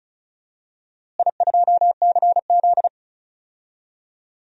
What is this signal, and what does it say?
Morse code, heard as a keyed tone, about 34 wpm, 700 Hz. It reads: I2CZ
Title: DXCC-100 Call Signs - 4 Characters - Repeated 2x - 35wpm